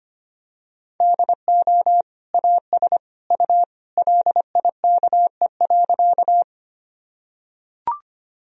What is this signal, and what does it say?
Morse code, heard as a keyed tone, about 25 wpm, 700 Hz. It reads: DO AS U LIKE.
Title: Sentences from Top 100 Words and Abbreviations 25wpm